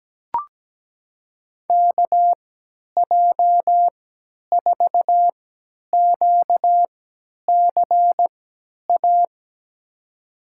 Morse code, heard as a keyed tone, about 17 wpm, spaced as for 10 wpm, 700 Hz. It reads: KJ4QCA
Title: USA Call Signs - 3x Character Spacing - 17wpm